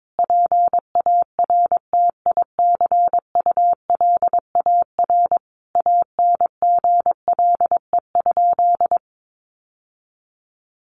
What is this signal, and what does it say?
Morse code, heard as a keyed tone, about 22 wpm, 700 Hz. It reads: PARTICULAR ANGLE?
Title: Sentences from Top 1500 Words 22wpm